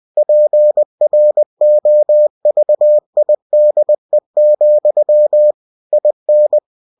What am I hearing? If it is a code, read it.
Morse code, heard as a keyed tone, about 20 wpm, 600 Hz. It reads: PROVIDE, IN
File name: cw-books / winnie-the-pooh / winnie-20wpm-page-0117